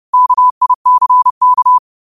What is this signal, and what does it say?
Morse code, heard as a keyed tone, about 30 wpm, 1000 Hz. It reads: KICK
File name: qotdfiles / 30wpm